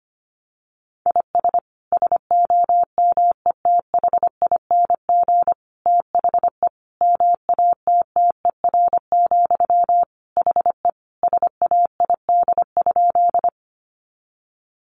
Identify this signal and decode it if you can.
Morse code, heard as a keyed tone, about 25 wpm, 700 Hz. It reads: IS SOMETHING THE MATTER, HE SAID?
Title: Sentences from Top 700 Words 25wpm